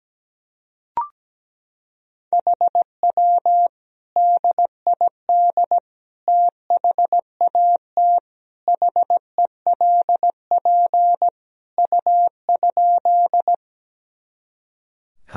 Morse code, heard as a keyed tone, about 17 wpm, 700 Hz. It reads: HW DID THAT HELP U?
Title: Sentences from Top 200 Words and Abbreviations 17wpm